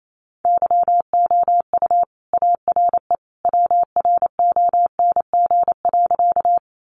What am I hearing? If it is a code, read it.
Morse code, heard as a keyed tone, about 28 wpm, 700 Hz. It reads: YOU ARE WRONG.
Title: Sentences from Top 900 Words 28wpm